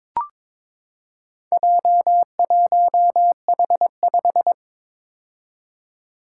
Morse code, heard as a keyed tone, about 22 wpm, 700 Hz. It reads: J1H5
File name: practice / 22wpm / Four Letter-Number Combinations - 22wpm